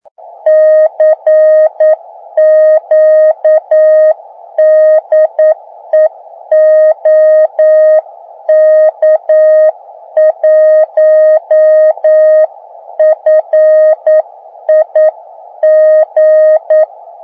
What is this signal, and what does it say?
Morse code, heard as a keyed tone, about 9 wpm, 620 Hz. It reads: CQ DE OK1FIG